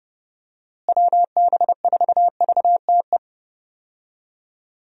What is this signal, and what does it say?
Morse code, heard as a keyed tone, about 30 wpm, 700 Hz. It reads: WB4VTE
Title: USA Call Signs - 6 Characters - Repeated 3x - 30wpm